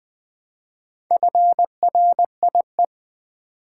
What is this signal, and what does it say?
Morse code, heard as a keyed tone, about 20 wpm, 690 Hz. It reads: FRIE
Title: Build A Word - 2 Letters At A Time - 6-Letter Words - 20wpm